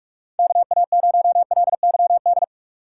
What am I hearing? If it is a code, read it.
Morse code, heard as a keyed tone, about 45 wpm, 700 Hz. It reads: KA0LYD
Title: USA Call Signs - 6 Characters - 4x Character Spacing - 45wpm